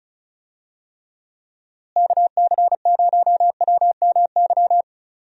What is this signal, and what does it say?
Morse code, heard as a keyed tone, about 34 wpm, 700 Hz. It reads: KC0WMY
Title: USA Call Signs - 6 Characters - Repeated 2x - 35wpm